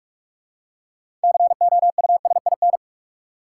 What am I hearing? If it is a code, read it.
Morse code, heard as a keyed tone, about 44 wpm, 700 Hz. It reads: COUSIN
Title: Six Letter Words - 45wpm